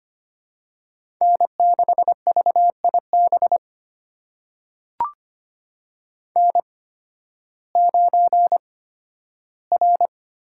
Morse code, heard as a keyed tone, about 25 wpm, spaced as for 13 wpm, 700 Hz. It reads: N6VIB N9R
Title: USA Call Signs - 8x Character Spacing - 25wpm